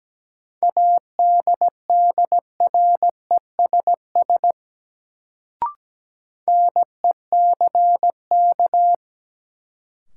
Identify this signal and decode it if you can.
Morse code, heard as a keyed tone, about 17 wpm, 700 Hz. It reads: ADDRESS NECK
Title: Sets of 5 Words Encore - New Word Drill - 17wpm